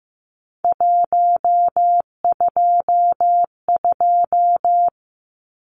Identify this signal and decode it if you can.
Morse code, heard as a keyed tone, about 15 wpm, 700 Hz. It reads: 122
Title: Numbers - 3-Digit 15wpm